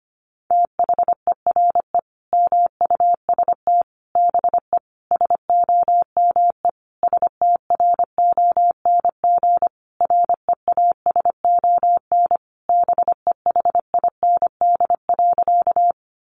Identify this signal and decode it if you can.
Morse code, heard as a keyed tone, about 25 wpm, 700 Hz. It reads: THERE MUST BE SOME STRONG REASON BEHIND.
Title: Sentences from Top 700 Words 25wpm